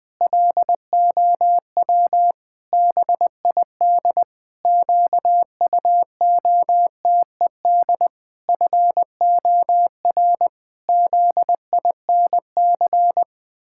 Morse code, heard as a keyed tone, about 20 wpm, 700 Hz. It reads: LOW BID QUOTED FOR ZINC
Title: Pangram Challenge v2 - 20wpm